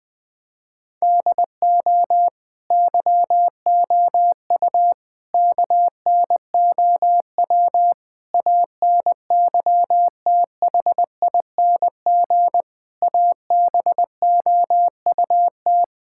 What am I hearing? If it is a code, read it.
Morse code, heard as a keyed tone, about 20 wpm, 700 Hz. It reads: DO YOU KNOW ANYTHING ABOUT
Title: Sentences from Top 1600 Words 20wpm